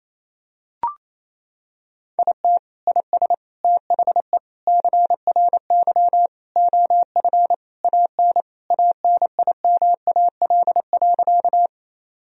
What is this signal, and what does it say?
Morse code, heard as a keyed tone, about 28 wpm, 700 Hz. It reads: IT IS THE CRY OF AN ANIMAL.
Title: Sentences from Top 1000 Words - Review Part 2 - 28wpm